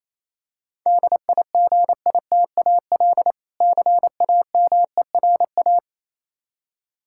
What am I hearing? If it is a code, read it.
Morse code, heard as a keyed tone, about 27 wpm, 700 Hz. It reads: DIGITAL CAMERA
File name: Easy Words - Speed-Racing with Hints - 22wpm